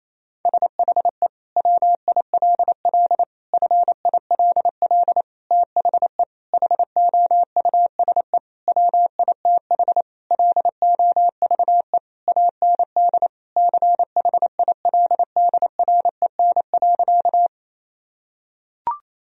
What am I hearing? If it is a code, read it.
Morse code, heard as a keyed tone, about 28 wpm, 700 Hz. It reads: SHE WILL FILL THE HOUSE WITH LOVE AND CHILDREN.
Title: Sentences from Top 1000 Words - Review Part 3 - 28wpm